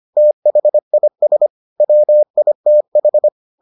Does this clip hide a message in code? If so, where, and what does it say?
Yes: Morse code from 0.2 to 3.3 s, heard as a keyed tone, about 25 wpm, 600 Hz. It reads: THIS WITH